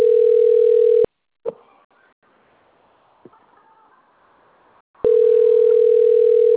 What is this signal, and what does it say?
Ringback tone (North America), 440 + 480 Hz.